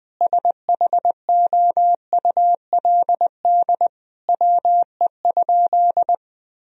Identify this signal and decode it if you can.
Morse code, heard as a keyed tone, about 21 wpm, 700 Hz. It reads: SHOULD WE?
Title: Sentences from Top 1000 Words - Review Part 3 - 20wpm